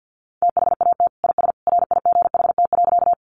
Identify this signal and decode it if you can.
Morse code, heard as a keyed tone, about 50 wpm, 700 Hz. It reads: THAT IS RIGHT.
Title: Sentences from Top 200 Words 50wpm